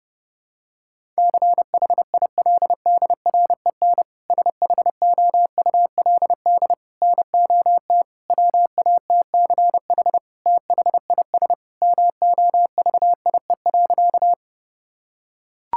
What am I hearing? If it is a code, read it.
Morse code, heard as a keyed tone, about 28 wpm, 700 Hz. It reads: CHILDREN SHOULD NOT WATCH THIS MOVIE.
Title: Sentences from Top 1000 Words - Review Part 5 - 30wpm